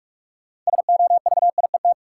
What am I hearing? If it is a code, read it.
Morse code, heard as a keyed tone, about 46 wpm, 700 Hz. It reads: SOVIET
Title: Sets of 5 Words Encore - New Word Drill - 45wpm